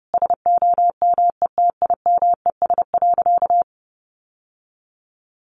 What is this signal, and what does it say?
Morse code, heard as a keyed tone, about 30 wpm, 700 Hz. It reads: SOMETIMES.
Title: Sentences from Top 1400 Words 30wpm